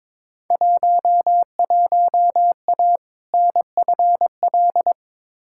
Morse code, heard as a keyed tone, about 22 wpm, 700 Hz. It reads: 11A NFL
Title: Field Day Exchanges Encore - 22wpm